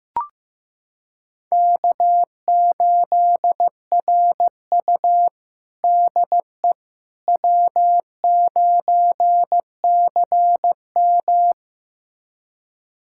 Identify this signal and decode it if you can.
Morse code, heard as a keyed tone, about 15 wpm, 700 Hz. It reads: K8RU DE W9CM
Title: USA Callsign DE Callsign - 4 Characters - 15wpm